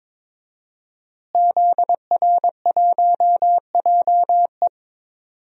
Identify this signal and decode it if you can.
Morse code, heard as a keyed tone, about 22 wpm, 700 Hz. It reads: ZR1JE
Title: DXCC-100 Call Signs - 5 Characters - Repeated 3x - 22wpm